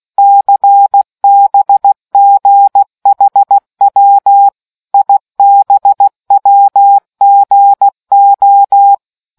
Morse code, heard as a keyed tone, about 16 wpm, 800 Hz. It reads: CBGHW IBWGO